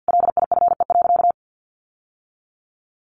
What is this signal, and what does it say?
Morse code, heard as a keyed tone, about 50 wpm, 700 Hz. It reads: LIFE.
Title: Sentences from Top 1200 Words 50wpm